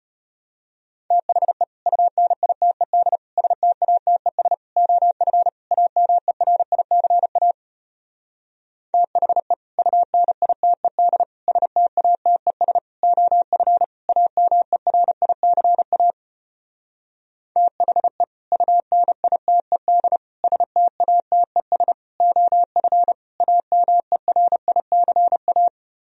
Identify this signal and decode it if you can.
Morse code, heard as a keyed tone, about 33 wpm, 700 Hz. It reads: THE UNITED STATES OF AMERICA THE UNITED STATES OF AMERICA THE UNITED STATES OF AMERICA